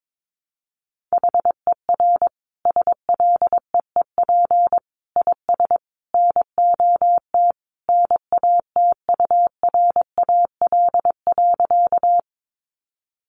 Morse code, heard as a keyed tone, about 22 wpm, 700 Hz. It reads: HER SLEEP IS NOT NATURAL.
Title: Sentences from Top 700 Words 22wpm